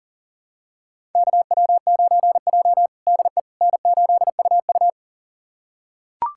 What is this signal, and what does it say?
Morse code, heard as a keyed tone, about 37 wpm, 700 Hz. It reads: KW9J DE N8UU
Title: USA Callsign DE Callsign - 4 Characters - 40wpm